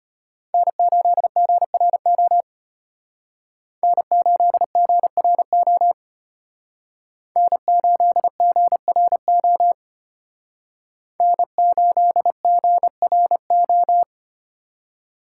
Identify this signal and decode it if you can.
Morse code, heard as a keyed tone, about 32 wpm, 700 Hz. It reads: N8GRO N8GRO N8GRO N8GRO